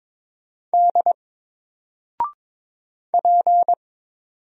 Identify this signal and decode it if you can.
Morse code, heard as a keyed tone, about 22 wpm, 700 Hz. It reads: D P
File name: Single-Letter-Number-22wpm